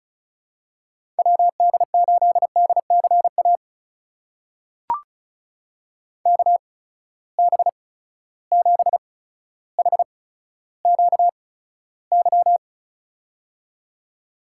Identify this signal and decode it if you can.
Morse code, heard as a keyed tone, about 36 wpm, spaced as for 15 wpm, 700 Hz. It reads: WD8DCA KB7HQY
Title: USA Call Signs - 6 Characters - 8x Character Spacing - 35wpm